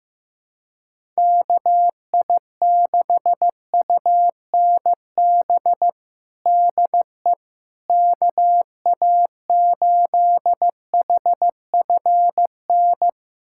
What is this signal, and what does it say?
Morse code, heard as a keyed tone, about 15 wpm, 700 Hz. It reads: KI6UNB DE KA8HFN
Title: USA Callsign DE Callsign - 6 Characters - 15wpm